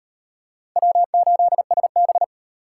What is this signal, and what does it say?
Morse code, heard as a keyed tone, about 35 wpm, 700 Hz. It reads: W8SB